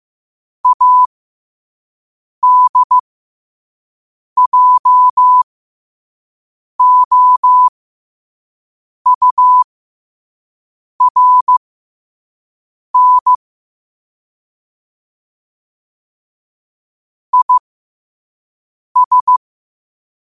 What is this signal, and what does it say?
Morse code, heard as a keyed tone, about 15 wpm, spaced as for 5 wpm, 1000 Hz. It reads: ADJOURN IS
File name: qotdfiles / 5wpm